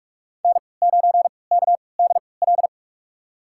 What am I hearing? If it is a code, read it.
Morse code, heard as a keyed tone, about 46 wpm, spaced as for 25 wpm, 700 Hz. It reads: N9KDL
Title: USA Call Signs - 5 Characters - 3x Character Spacing - 45wpm